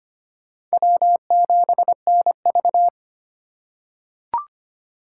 Morse code, heard as a keyed tone, about 25 wpm, 700 Hz. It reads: W7NV
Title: USA Call Signs - 4 Characters - 3x Character Spacing - 25wpm